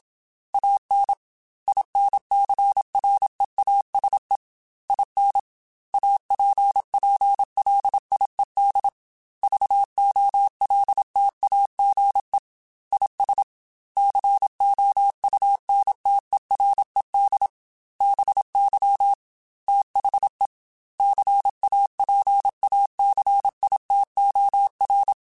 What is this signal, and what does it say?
Morse code, heard as a keyed tone, about 27 wpm, 800 Hz. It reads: AN INCREASE IN APPLIED VOLTAGE IS COUNTERED BY THE CAPACITOR
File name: Technical Subjects / Technical Subjects-mp3s / XC @25